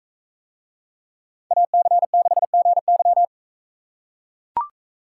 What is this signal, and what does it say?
Morse code, heard as a keyed tone, about 42 wpm, 700 Hz. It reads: AC6GY